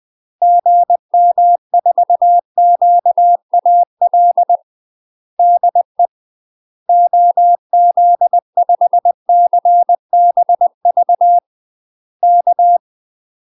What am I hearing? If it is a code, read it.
Morse code, heard as a keyed tone, about 20 wpm, 700 Hz. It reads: GM4QAL DE OZ5CBV K
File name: UK MASTER QSOs TRAINING / 20WPM / UK MASTER QSOs 0020